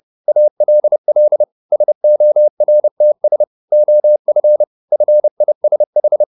Morse code, heard as a keyed tone, about 30 wpm, 600 Hz. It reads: ALL SORTS OF FISH
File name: the old man and the sea / the old man and the sea 1 30wpm